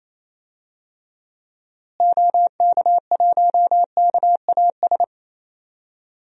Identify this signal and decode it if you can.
Morse code, heard as a keyed tone, about 28 wpm, 700 Hz. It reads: OK1KAS
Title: DXCC-100 Call Signs - 6 Characters - Repeated 3x - 28wpm